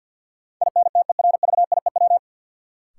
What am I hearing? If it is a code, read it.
Morse code, heard as a keyed tone, about 53 wpm, 700 Hz. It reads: INTERVIEW
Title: Sentences from Top 1600 Words - New Word Drill - 50wpm